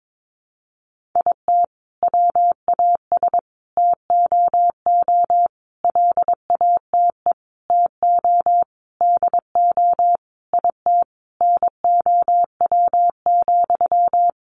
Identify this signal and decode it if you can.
Morse code, heard as a keyed tone, about 22 wpm, 700 Hz. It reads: IT WAS TOO LATE TO DO IT NOW,
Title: Sentences from Top 600 Words 22wpm